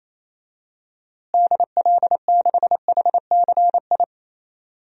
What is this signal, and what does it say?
Morse code, heard as a keyed tone, about 28 wpm, 700 Hz. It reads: DL6HCI
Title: DXCC-100 Call Signs - 6 Characters - Repeated 3x - 28wpm